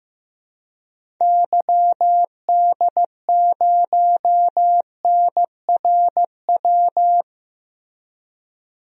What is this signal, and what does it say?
Morse code, heard as a keyed tone, about 15 wpm, 700 Hz. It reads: YD0NRW
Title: DXCC-100 Call Signs - 6 Characters - Repeated 3x - 15wpm